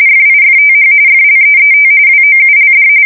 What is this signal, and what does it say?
RTTY, about 45.45 baud. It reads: FIFTH FLEET WOULD
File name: fsk170 / USS-New-Jersey